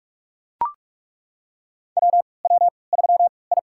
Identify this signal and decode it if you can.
Morse code, heard as a keyed tone, about 41 wpm, spaced as for 25 wpm, 690 Hz. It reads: WW3I
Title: USA Call Signs - 4 Characters - 3x Character Spacing - 45wpm